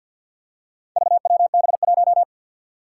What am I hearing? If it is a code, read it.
Morse code, heard as a keyed tone, about 51 wpm, 700 Hz. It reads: VWB1
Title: Four Challenging Letter-Number Combinations 50wpm